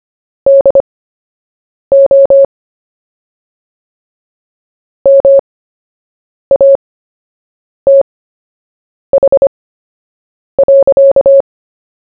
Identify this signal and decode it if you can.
Morse code, heard as a keyed tone, about 25 wpm, spaced as for 7 wpm, 550 Hz. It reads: DO MATH.
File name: cw / fists / Short-Story-1-7wpm